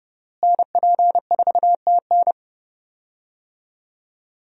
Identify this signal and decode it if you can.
Morse code, heard as a keyed tone, about 30 wpm, 700 Hz. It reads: NP4TN
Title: USA Call Signs - 5 Characters - Repeated 2x - 30wpm